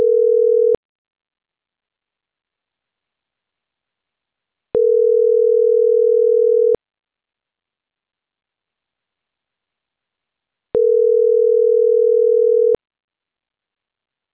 Ringback tone (North America), 440 + 480 Hz.